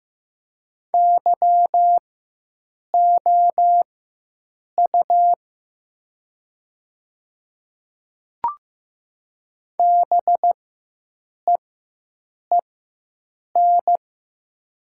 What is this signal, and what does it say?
Morse code, heard as a keyed tone, about 15 wpm, spaced as for 7 wpm, 690 Hz. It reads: YOU BEEN
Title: 4x Character Spacing - Top 200 Words - 15wpm